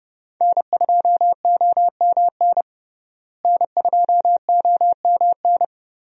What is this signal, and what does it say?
Morse code, heard as a keyed tone, about 32 wpm, 700 Hz. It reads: N2OMN N2OMN
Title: USA Call Signs - 5 Characters - Repeated 3x - 30wpm